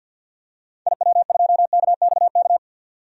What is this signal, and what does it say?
Morse code, heard as a keyed tone, about 47 wpm, 700 Hz. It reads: IW2KXK
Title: DXCC-100 Call Signs - 6 Characters - 50wpm